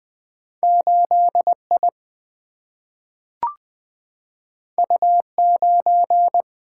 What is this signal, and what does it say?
Morse code, heard as a keyed tone, about 20 wpm, 700 Hz. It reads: 8 I U 9
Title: Single-Letter Single-Number 20wpm